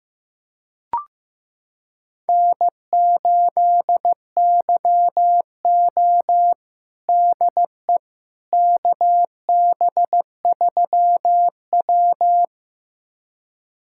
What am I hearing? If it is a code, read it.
Morse code, heard as a keyed tone, about 15 wpm, 700 Hz. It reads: N8YO DE KB3W
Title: USA Callsign DE Callsign - 4 Characters - 15wpm